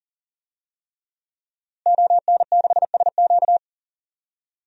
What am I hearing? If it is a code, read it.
Morse code, heard as a keyed tone, about 40 wpm, 700 Hz. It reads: ON6SQ
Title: DXCC-100 Call Signs - 5 Characters - Repeated 3x - 40wpm